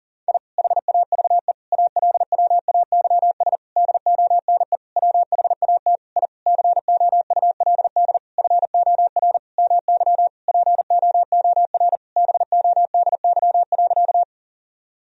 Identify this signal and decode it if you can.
Morse code, heard as a keyed tone, about 37 wpm, 700 Hz. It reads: I HAVE ALWAYS DONE WHAT I COULD FOR MY POOR BODY.
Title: Sentences from Top 1000 Words - Review Part 2 - 40wpm